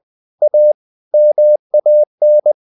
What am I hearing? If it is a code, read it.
Morse code, heard as a keyed tone, about 20 wpm, 600 Hz. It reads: A MAN
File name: the old man and the sea part 2 20wpm